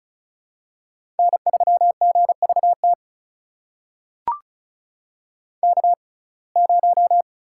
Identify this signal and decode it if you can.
Morse code, heard as a keyed tone, about 36 wpm, spaced as for 25 wpm, 700 Hz. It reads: N3GVT K0
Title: USA Call Signs - 5 Characters - 6x Character Spacing - 35wpm